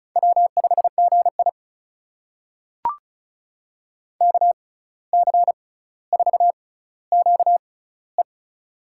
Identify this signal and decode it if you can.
Morse code, heard as a keyed tone, about 36 wpm, spaced as for 17 wpm, 700 Hz. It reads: W5GI KC4QE
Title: USA Call Signs - 6x Character Spacing - 35wpm